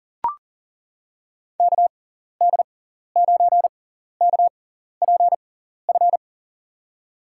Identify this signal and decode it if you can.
Morse code, heard as a keyed tone, about 37 wpm, spaced as for 14 wpm, 700 Hz. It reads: KD9KPF
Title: USA Call Signs - 6 Characters - 6x Character Spacing - 40wpm